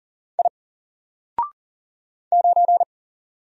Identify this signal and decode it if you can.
Morse code, heard as a keyed tone, about 40 wpm, 700 Hz. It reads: I 9
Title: Single Letter-Number 40wpm